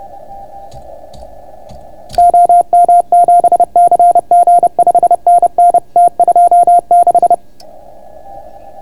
Morse code, heard as a keyed tone, about 31 wpm, 690 Hz. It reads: OM7CG 5NN T26